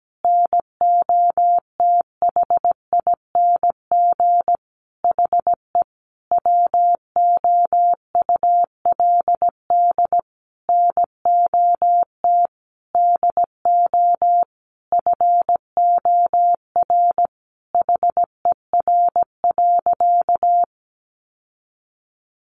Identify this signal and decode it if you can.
Morse code, heard as a keyed tone, about 17 wpm, 700 Hz. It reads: NOTHING HE WOULD NOT DO FOR HER.
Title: Sentences from Top 600 Words 17wpm